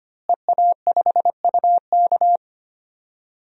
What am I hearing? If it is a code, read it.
Morse code, heard as a keyed tone, about 25 wpm, 700 Hz. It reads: EA5UK
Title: DXCC-100 Call Signs - 5 Characters - Repeated 3x - 25wpm